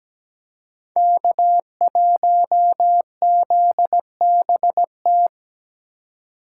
Morse code, heard as a keyed tone, about 17 wpm, 700 Hz. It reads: K1ZBT
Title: USA Call Signs - 5 Characters - 4x Character Spacing - 17wpm